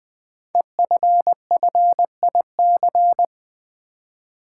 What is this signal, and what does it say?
Morse code, heard as a keyed tone, about 20 wpm, 700 Hz. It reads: EFFIC
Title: Build A Word - 5 Letters At A Time - 10-Letter Words - 20wpm